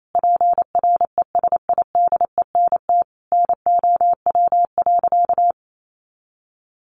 Morse code, heard as a keyed tone, about 28 wpm, 700 Hz. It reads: PRESIDENT NOW.
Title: A Sound Of Thunder 28wpm